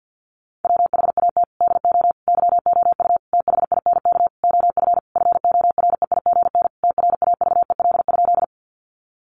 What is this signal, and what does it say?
Morse code, heard as a keyed tone, about 50 wpm, 700 Hz. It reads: WHAT DO YOU THINK OF FOREIGN TRAVEL?
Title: Sentences from Top 1400 Words 50wpm